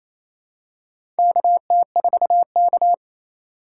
Morse code, heard as a keyed tone, about 28 wpm, 700 Hz. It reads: KT4K
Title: USA Call Signs - 4x Character Spacing - 28wpm